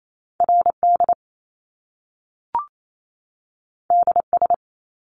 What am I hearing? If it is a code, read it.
Morse code, heard as a keyed tone, about 28 wpm, 700 Hz. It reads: RD DS
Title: Top Two Letter Combinations in Words 28wpm